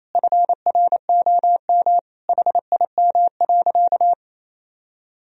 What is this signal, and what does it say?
Morse code, heard as a keyed tone, about 28 wpm, 700 Hz. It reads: FROM HIM.
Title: Sentences from Top 1000 Words - Review Part 2 - 28wpm